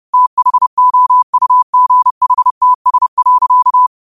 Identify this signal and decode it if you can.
Morse code, heard as a keyed tone, about 30 wpm, 1000 Hz. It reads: THOUGHTS.